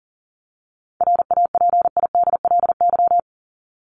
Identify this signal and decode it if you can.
Morse code, heard as a keyed tone, about 40 wpm, 700 Hz. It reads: RAPIDLY
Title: LY Suffix - Top 100 Words with Base Words 40wpm